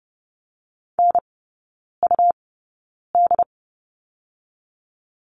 Morse code, heard as a keyed tone, about 30 wpm, spaced as for 9 wpm, 700 Hz. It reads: NUD
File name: Lesson 18e - Letter m - Three Characters 30wpm